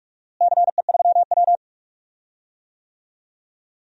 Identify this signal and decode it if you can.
Morse code, heard as a keyed tone, about 41 wpm, 700 Hz. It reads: CE3W
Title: DXCC-100 Call Signs - 4 Characters - Repeated 3x - 45wpm